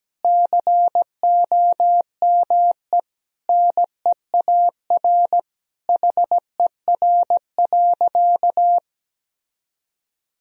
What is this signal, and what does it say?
Morse code, heard as a keyed tone, about 17 wpm, 700 Hz. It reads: COME NEAR HER.
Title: Sentences from Top 1000 Words - Review Part 1 - 17wpm